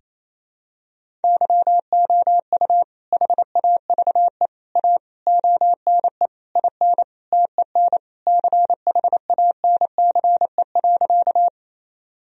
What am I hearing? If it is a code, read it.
Morse code, heard as a keyed tone, about 28 wpm, 700 Hz. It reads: YOU HAVE A ONE IN TEN CHANCE.
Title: Sentences from Top 1000 Words - Review Part 2 - 28wpm